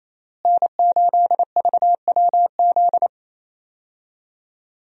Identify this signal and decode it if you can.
Morse code, heard as a keyed tone, about 27 wpm, 700 Hz. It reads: N8VWZ
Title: USA Call Signs - 5 Characters - Repeated 3x - 28wpm